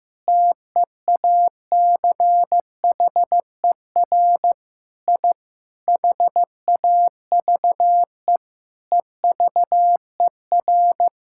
Morse code, heard as a keyed tone, about 15 wpm, 700 Hz. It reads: TEACHER I HAVE EVER
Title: Sentences from Top 1000 Words - Review Part 6 - 15wpm